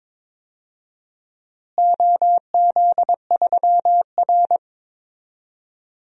Morse code, heard as a keyed tone, about 22 wpm, 700 Hz. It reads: OZ3R
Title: DXCC-100 Call Signs - 4 Characters - Repeated 2x - 22wpm